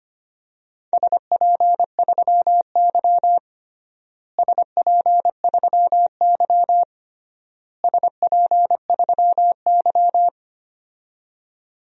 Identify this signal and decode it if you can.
Morse code, heard as a keyed tone, about 25 wpm, 700 Hz. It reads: SP3Y SP3Y SP3Y
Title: DXCC-100 Call Signs - 4 Characters - Repeated 3x - 25wpm